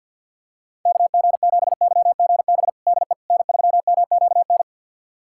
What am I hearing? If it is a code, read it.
Morse code, heard as a keyed tone, about 50 wpm, 700 Hz. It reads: KG7YGB DE N3MQN
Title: USA Callsign DE Callsign 50wpm